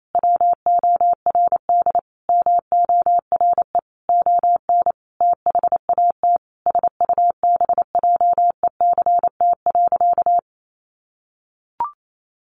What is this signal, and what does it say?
Morse code, heard as a keyed tone, about 28 wpm, 700 Hz. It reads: WORD MORE ON THAT SUBJECT.
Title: Sentences from Top 800 Words 28wpm